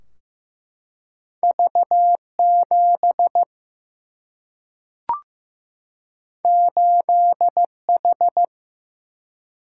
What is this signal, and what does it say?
Morse code, heard as a keyed tone, about 15 wpm, 700 Hz. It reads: V 7 8 H
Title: Single-Letter Single-Number 15wpm